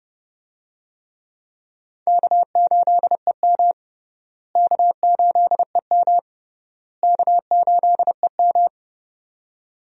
Morse code, heard as a keyed tone, about 30 wpm, 700 Hz. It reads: K8EM K8EM K8EM
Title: USA Call Signs - 4 Characters - Repeated 3x - 30wpm